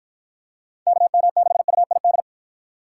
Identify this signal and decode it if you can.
Morse code, heard as a keyed tone, about 47 wpm, 700 Hz. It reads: KM6UID